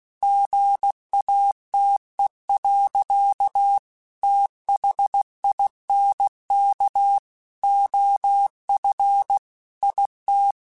Morse code, heard as a keyed tone, about 16 wpm, 800 Hz. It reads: GATE. THINK OF IT
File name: Technical Subjects / Technical Subjects-mp3s / OR Gate @15